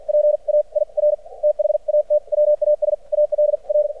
Morse code, heard as a keyed tone, about 50 wpm, spaced as for 42 wpm, 600 Hz. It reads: PAIR THAT WAS ALR
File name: archived / CRSnet